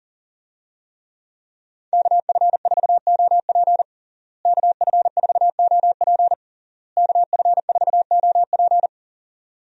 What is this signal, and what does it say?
Morse code, heard as a keyed tone, about 40 wpm, 700 Hz. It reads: KF4OP KF4OP KF4OP
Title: USA Call Signs - 5 Characters - Repeated 3x - 40wpm